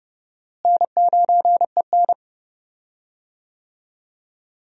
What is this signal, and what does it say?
Morse code, heard as a keyed tone, about 32 wpm, 700 Hz. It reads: N9EN
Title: USA Call Signs - 4 Characters - Repeated 3x - 30wpm